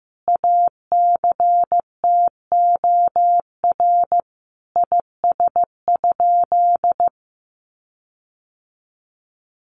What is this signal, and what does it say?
Morse code, heard as a keyed tone, about 15 wpm, 700 Hz. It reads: ACTOR IS?
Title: Sentences from Top 1500 Words 15wpm